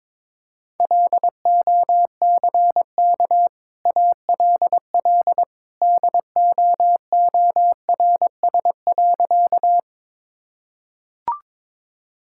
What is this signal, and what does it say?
Morse code, heard as a keyed tone, about 22 wpm, 690 Hz. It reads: LOCK ALL DOORS.
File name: practice / 22wpm / Easy Sentences with Hints - 22wpm